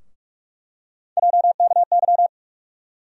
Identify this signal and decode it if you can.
Morse code, heard as a keyed tone, about 44 wpm, 700 Hz. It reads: JKY
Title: USA Call Sign Postfixes - 3 Letters 45wpm